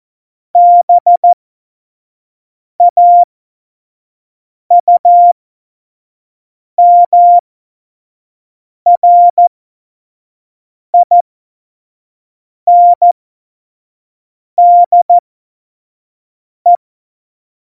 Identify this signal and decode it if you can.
Morse code, heard as a keyed tone, about 14 wpm, spaced as for 5 wpm, 700 Hz. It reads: BAUMRINDE
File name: Morsekurs / Witze / Witz 17.Dezember 2019, 25 F70